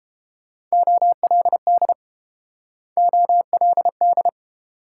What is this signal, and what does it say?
Morse code, heard as a keyed tone, about 29 wpm, 700 Hz. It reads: OLD OLD